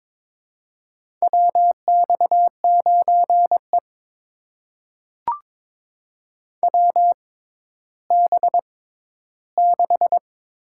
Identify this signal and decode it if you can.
Morse code, heard as a keyed tone, about 21 wpm, spaced as for 12 wpm, 700 Hz. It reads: WX9E WB6
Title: USA Call Signs - 4 Characters - 6x Character Spacing - 22wpm